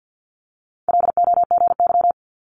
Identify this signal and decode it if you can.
Morse code, heard as a keyed tone, about 50 wpm, 700 Hz. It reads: LOGY
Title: Multi-letter Phonemes Quick Fire 50wpm